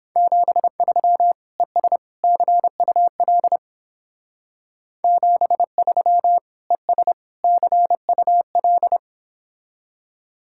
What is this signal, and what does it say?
Morse code, heard as a keyed tone, about 27 wpm, 700 Hz. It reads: 73 ES CUL 73 ES CUL
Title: Speed-Racing Common QSO Phrases - 22wpm